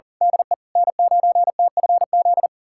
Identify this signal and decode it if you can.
Morse code, heard as a keyed tone, about 40 wpm, 700 Hz. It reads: DE N9TFZ
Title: USA Callsign DE Callsign - 5 Characters - 40wpm